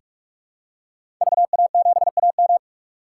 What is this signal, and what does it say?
Morse code, heard as a keyed tone, about 46 wpm, 700 Hz. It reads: VA7AM
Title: K1USN SST QSO Practice - 45wpm